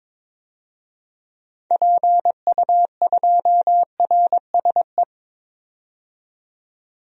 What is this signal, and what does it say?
Morse code, heard as a keyed tone, about 22 wpm, 700 Hz. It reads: PU2RSE
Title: DXCC-100 Call Signs - 6 Characters - 22wpm